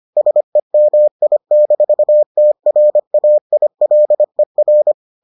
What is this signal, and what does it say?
Morse code, heard as a keyed tone, about 25 wpm, 600 Hz. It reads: SEMI-TRAILER